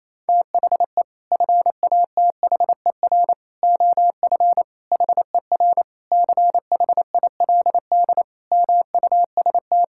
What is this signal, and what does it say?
Morse code, heard as a keyed tone, about 28 wpm, 700 Hz. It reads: THE FATHER OF HER CHILD MUST
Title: Sentences from Top 1000 Words - Review Part 5 - 28wpm